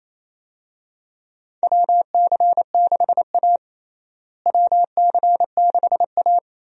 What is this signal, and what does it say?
Morse code, heard as a keyed tone, about 28 wpm, 700 Hz. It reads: WC6A WC6A
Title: USA Call Signs - 4 Characters - Repeated 3x - 28wpm